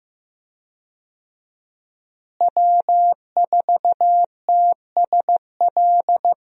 Morse code, heard as a keyed tone, about 16 wpm, 700 Hz. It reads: W4TSL
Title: USA Call Signs - 5 Characters - Repeated 2x - 15wpm